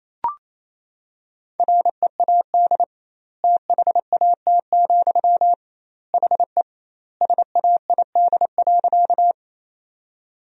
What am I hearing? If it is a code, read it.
Morse code, heard as a keyed tone, about 27 wpm, 700 Hz. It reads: READ THAT, HE SAID.
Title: 2x Word-Spacing - Sentences from Top 200 Words - 28wpm